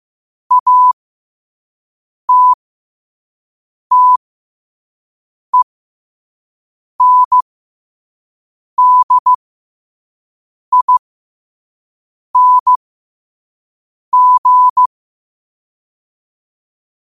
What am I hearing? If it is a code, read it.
Morse code, heard as a keyed tone, about 15 wpm, spaced as for 5 wpm, 1000 Hz. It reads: ATTENDING
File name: 5wpm